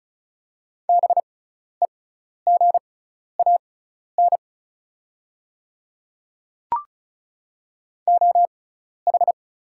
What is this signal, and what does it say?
Morse code, heard as a keyed tone, about 36 wpm, spaced as for 12 wpm, 700 Hz. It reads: BEGAN OH
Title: 6x Character Spacing - Top 500 Words - 35wpm